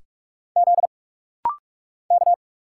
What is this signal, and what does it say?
Morse code, heard as a keyed tone, about 46 wpm, 700 Hz. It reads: Z K
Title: Single Letters - Rapid-Fire 45wpm